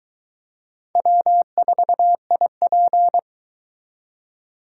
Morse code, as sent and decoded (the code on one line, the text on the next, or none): .-- ....- .. .--.
W4IP